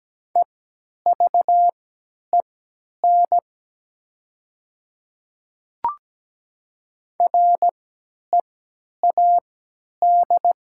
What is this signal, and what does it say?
Morse code, heard as a keyed tone, about 17 wpm, spaced as for 10 wpm, 700 Hz. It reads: EVEN READ
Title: 3x Character Spacing - Top 200 Words - 17wpm